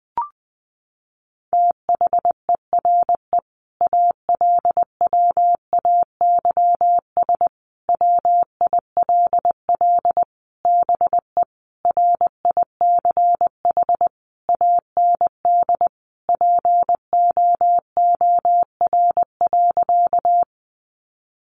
Morse code, heard as a keyed tone, about 20 wpm, 700 Hz. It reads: THERE ALWAYS WILL BE RICH AND POOR.
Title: Sentences from Top 1100 Words 20wpm